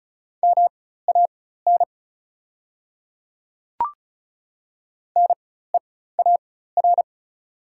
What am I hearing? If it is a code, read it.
Morse code, heard as a keyed tone, about 36 wpm, spaced as for 16 wpm, 700 Hz. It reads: MAN NEAR
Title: 4x Character Spacing - Top 500 Words - 35wpm